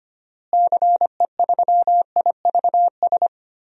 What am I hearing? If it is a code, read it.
Morse code, heard as a keyed tone, about 25 wpm, 700 Hz. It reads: CE3IVS
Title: DXCC-100 Call Signs - 6 Characters - Repeated 2x - 25wpm